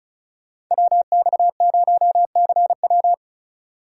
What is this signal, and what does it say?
Morse code, heard as a keyed tone, about 37 wpm, 700 Hz. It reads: WX0CW
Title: Speed Racing - USA Call Signs 30wpm